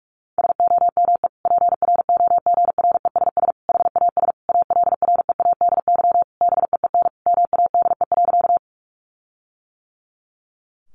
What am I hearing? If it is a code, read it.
Morse code, heard as a keyed tone, about 44 wpm, 700 Hz. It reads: SOME PROGRESS HAS ALREADY BEEN MADE.
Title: Sentences from Top 1400 Words 45wpm